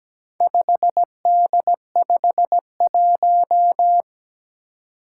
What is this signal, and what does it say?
Morse code, heard as a keyed tone, about 17 wpm, 700 Hz. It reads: 5D51
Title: Four Challenging Letter-Number Combinations 17wpm